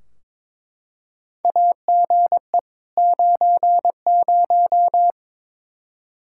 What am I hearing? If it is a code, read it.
Morse code, heard as a keyed tone, about 22 wpm, 700 Hz. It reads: AGE 90
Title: Practice QSO - Exchange 3 - More Info - 22wpm